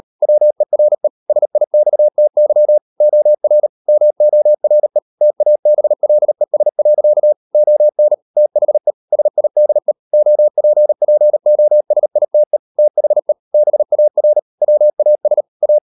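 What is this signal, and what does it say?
Morse code, heard as a keyed tone, about 38 wpm, 600 Hz. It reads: WERE SIXTY OR MORE TABLES. ON THE SIDE OPPOSITE THE BAR WAS A
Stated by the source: Morse code practice at 38 WPM, sending Ken Follett's novel Never in English, from CW med Gnister.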